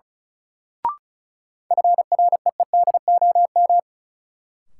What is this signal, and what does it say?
Morse code, heard as a keyed tone, about 34 wpm, 700 Hz. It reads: FREEDOM
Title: Sentences from Top 800 Words - New Word Drill - 35wpm